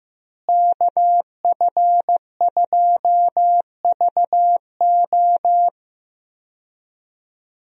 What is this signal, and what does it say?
Morse code, heard as a keyed tone, about 15 wpm, 700 Hz. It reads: KF2VO
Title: USA Call Signs - 5 Characters - Repeated 3x - 15wpm